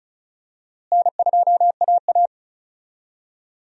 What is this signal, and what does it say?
Morse code, heard as a keyed tone, about 36 wpm, 700 Hz. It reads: N2AA
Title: USA Call Signs - 4 Characters - Repeated 3x - 35wpm